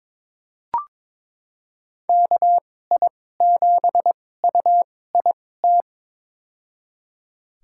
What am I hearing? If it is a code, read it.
Morse code, heard as a keyed tone, about 22 wpm, spaced as for 16 wpm, 700 Hz. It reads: KI7UIT
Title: USA Call Signs - 6 Characters - 2x Character Spacing - 22wpm